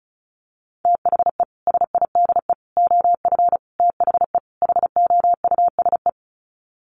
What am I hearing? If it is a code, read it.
Morse code, heard as a keyed tone, about 34 wpm, 700 Hz. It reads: THE SIDE OF THE HOUSE
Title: Sets of 5 Words Encore2 35wpm